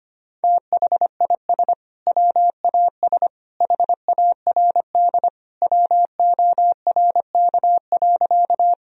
Morse code, heard as a keyed tone, about 25 wpm, 700 Hz. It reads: THIS WAS HARD WORK.
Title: Sentences from Top 1000 Words - Review Part 1 - 25wpm